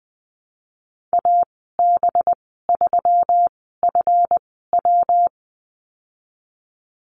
Morse code, heard as a keyed tone, about 20 wpm, spaced as for 15 wpm, 700 Hz. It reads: AB3FW
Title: 193 Lesson 31g - Letter v - Call Signs 20wpm